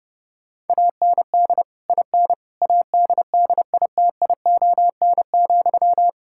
Morse code, heard as a keyed tone, about 32 wpm, 700 Hz. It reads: AND IN ADDITION,
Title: Sentences from Top 1000 Words - Review Part 6 - 30wpm